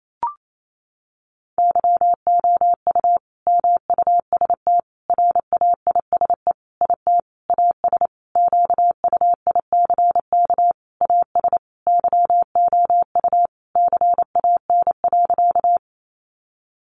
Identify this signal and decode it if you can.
Morse code, heard as a keyed tone, about 28 wpm, 700 Hz. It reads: YOU MUST RAISE IT AS QUICK AS YOU CAN.
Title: Sentences from Top 1400 Words 28wpm